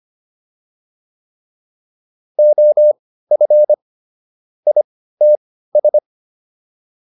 Morse code, heard as a keyed tone, about 25 wpm, spaced as for 15 wpm, 600 Hz. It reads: OF ITS